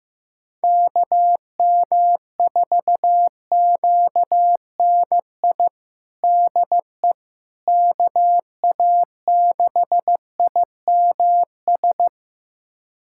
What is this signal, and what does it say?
Morse code, heard as a keyed tone, about 15 wpm, 700 Hz. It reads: KM4QNI DE KA6IMS
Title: USA Callsign DE Callsign - 6 Characters - 15wpm